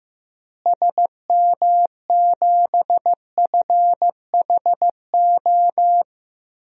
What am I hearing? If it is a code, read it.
Morse code, heard as a keyed tone, about 15 wpm, 700 Hz. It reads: SM7FHO
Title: DXCC-100 Call Signs - 6 Characters - Repeated 3x - 15wpm